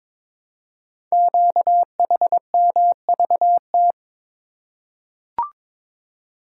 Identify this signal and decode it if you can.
Morse code, heard as a keyed tone, about 22 wpm, 700 Hz. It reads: QHMVT
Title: Code Groups - 5 Letters - 22wpm